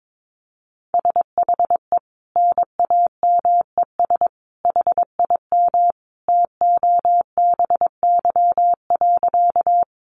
Morse code, heard as a keyed tone, about 22 wpm, 700 Hz. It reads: SHE NAMES HIM TOBY.
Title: Easy Phrases 22wpm